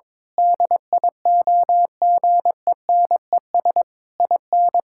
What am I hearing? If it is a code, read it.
Morse code, heard as a keyed tone, about 22 wpm, 700 Hz. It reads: DIOGENES IN
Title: Pangram Challenge v2 - 22wpm